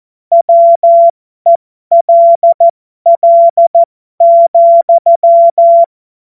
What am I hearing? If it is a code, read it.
Morse code, heard as a keyed tone, about 14 wpm, spaced as for 12 wpm, 680 Hz. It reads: WELL,
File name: Morsekurs / Witze